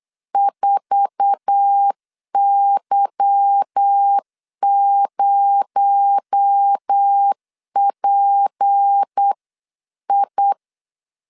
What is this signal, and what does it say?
Morse code, heard as a keyed tone, about 8 wpm, 800 Hz. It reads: 4Y0PI
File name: LEZIONI / 10 SIGNS / 10A SLASH 8wpm 3452